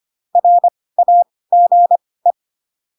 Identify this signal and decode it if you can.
Morse code, heard as a keyed tone, about 25 wpm, spaced as for 18 wpm, 700 Hz. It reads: RAGE
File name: Pinocchio / Pinocchio ch2 12 25 ik1hge